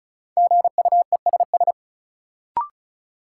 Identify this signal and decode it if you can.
Morse code, heard as a keyed tone, about 36 wpm, 700 Hz. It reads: GUESS
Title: Sentences from Top 1500 Words - New Word Drill - 35wpm